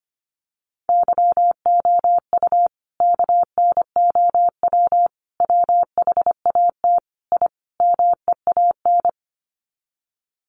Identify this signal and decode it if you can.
Morse code, heard as a keyed tone, about 25 wpm, 700 Hz. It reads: YOU KNOW WHAT I MEAN
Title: Sets of 5 Words 25wpm